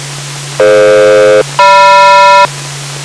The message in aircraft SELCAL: DF-GM